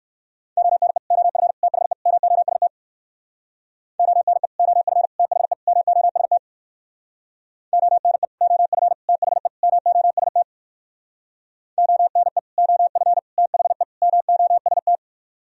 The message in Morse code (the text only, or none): ONE OF THE MOST ONE OF THE MOST ONE OF THE MOST ONE OF THE MOST